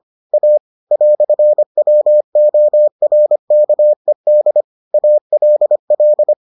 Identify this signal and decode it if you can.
Morse code, heard as a keyed tone, about 25 wpm, 600 Hz. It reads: A "WORKED ALL